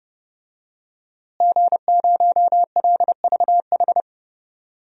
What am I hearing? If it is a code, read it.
Morse code, heard as a keyed tone, about 31 wpm, 700 Hz. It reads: G0LVH
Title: DXCC-100 Call Signs - 5 Characters - Repeated 2x - 30wpm